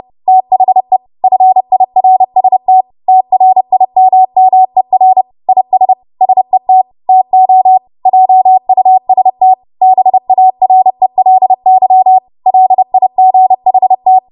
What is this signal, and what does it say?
Morse code, heard as a keyed tone, about 30 wpm, 750 Hz. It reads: THE FIRST TRIMMER IS SET TO JUST BARELY LIGHT
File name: Archive / 30 WPM / 30WPM